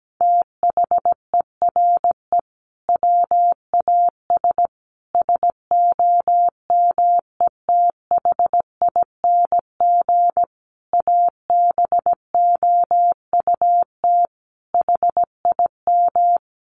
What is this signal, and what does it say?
Morse code, heard as a keyed tone, about 17 wpm, 700 Hz. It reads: THERE WAS SOMETHING ABOUT HIM
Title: Sentences from Top 1500 Words 17wpm